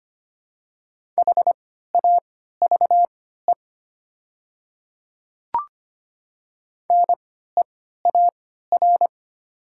Morse code, heard as a keyed tone, about 25 wpm, spaced as for 14 wpm, 700 Hz. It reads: HAVE NEAR
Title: 3x Character Spacing - Top 300 Words - 25wpm